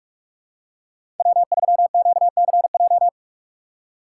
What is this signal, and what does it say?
Morse code, heard as a keyed tone, about 48 wpm, 700 Hz. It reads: W3QCJ
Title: USA Call Signs - 2x Character Spacing - 45wpm